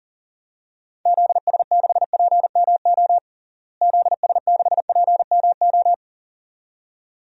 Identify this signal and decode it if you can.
Morse code, heard as a keyed tone, about 40 wpm, 700 Hz. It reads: ZS6PMO ZS6PMO
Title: DXCC-100 Call Signs - 6 Characters - Repeated 2x - 40wpm